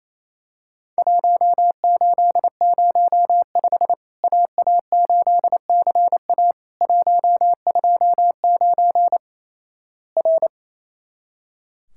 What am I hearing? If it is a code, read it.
Morse code, heard as a keyed tone, about 28 wpm, 700 Hz. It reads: 1805 AA8CA 129 R
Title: Worked All Europe DX Contest - QTC Exchange - 28wpm